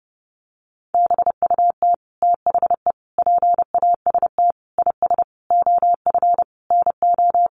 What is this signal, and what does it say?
Morse code, heard as a keyed tone, about 30 wpm, 700 Hz. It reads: BUT THE PAST IS OF NO
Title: Sentences from Top 1200 Words 30wpm